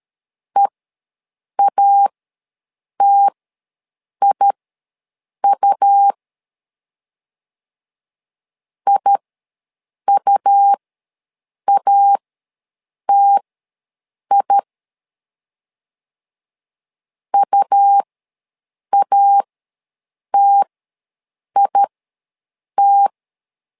Morse code, heard as a keyed tone, about 13 wpm, spaced as for 7 wpm, 800 Hz. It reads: EATIU IUATI UATIT
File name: LEZIONI / 2 UHDN / 2 UH ws 12wpm 1730